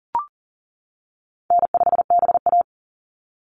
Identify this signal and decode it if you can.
Morse code, heard as a keyed tone, about 40 wpm, 700 Hz. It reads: N5BA
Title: Call Signs - 40wpm